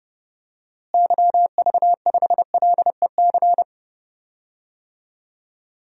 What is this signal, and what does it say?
Morse code, heard as a keyed tone, about 32 wpm, 700 Hz. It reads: YV5LEC
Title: DXCC-100 Call Signs - 6 Characters - Repeated 3x - 30wpm